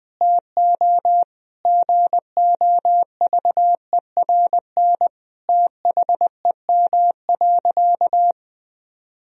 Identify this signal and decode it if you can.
Morse code, heard as a keyed tone, about 20 wpm, 700 Hz. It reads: TO GOVERN THEM.
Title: Sentences from Top 1000 Words - Review Part 2 - 20wpm